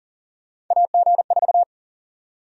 Morse code, heard as a keyed tone, about 40 wpm, 700 Hz. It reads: AG4
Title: USA Call Sign Prefixes - 40wpm